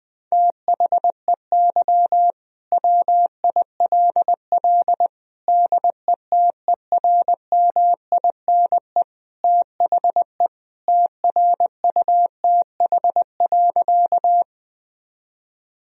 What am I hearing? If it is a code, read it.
Morse code, heard as a keyed tone, about 20 wpm, 700 Hz. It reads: THEY WILL DETERMINE THE TRUTH.
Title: Sentences from Top 1000 Words - Review Part 4 - 20wpm